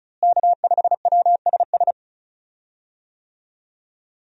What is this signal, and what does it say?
Morse code, heard as a keyed tone, about 34 wpm, 700 Hz. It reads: K5WSS
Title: USA Call Signs - 5 Characters - Repeated 2x - 35wpm